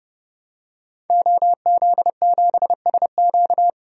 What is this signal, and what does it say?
Morse code, heard as a keyed tone, about 30 wpm, 700 Hz. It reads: OZ7SQ
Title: DXCC-100 Call Signs - 5 Characters - 30wpm